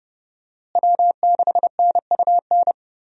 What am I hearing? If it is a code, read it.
Morse code, heard as a keyed tone, about 32 wpm, 700 Hz. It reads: W6NUN
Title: USA Call Signs - 5 Characters - Repeated 2x - 30wpm